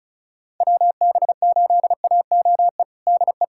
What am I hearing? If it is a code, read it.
Morse code, heard as a keyed tone, about 36 wpm, 700 Hz. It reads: WB8AOE DE
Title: USA Callsign DE Callsign - 6 Characters - 35wpm